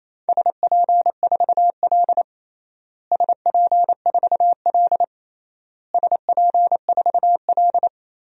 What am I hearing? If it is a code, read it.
Morse code, heard as a keyed tone, about 28 wpm, 700 Hz. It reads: SP4L SP4L SP4L
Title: DXCC-100 Call Signs - 4 Characters - Repeated 3x - 28wpm